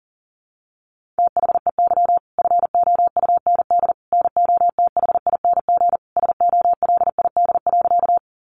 Morse code, heard as a keyed tone, about 40 wpm, 700 Hz. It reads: THEY FOUND NOTHING SOLID.
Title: Sentences from Top 1500 Words 40wpm